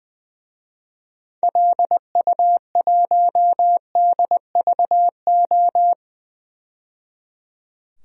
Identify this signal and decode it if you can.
Morse code, heard as a keyed tone, about 20 wpm, 700 Hz. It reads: LU1DVO
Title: DXCC-100 Call Signs - 6 Characters - 20wpm